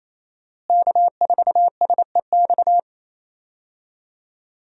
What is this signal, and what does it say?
Morse code, heard as a keyed tone, about 28 wpm, 700 Hz. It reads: K4SEX
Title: USA Call Signs - 5 Characters - Repeated 3x - 28wpm